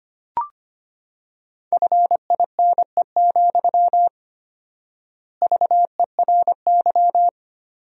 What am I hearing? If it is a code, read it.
Morse code, heard as a keyed tone, about 25 wpm, 700 Hz. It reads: FINE, VERY
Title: 4x Word-Spacing - Sentences from Top 500 Words - 25wpm